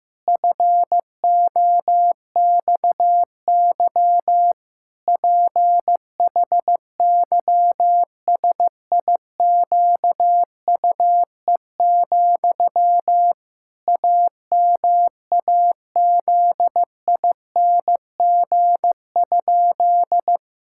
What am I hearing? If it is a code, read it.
Morse code, heard as a keyed tone, about 16 wpm, 700 Hz. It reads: FOXY PHYSIQUE, AMAZING?
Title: Pangram Challenge v2 - 15wpm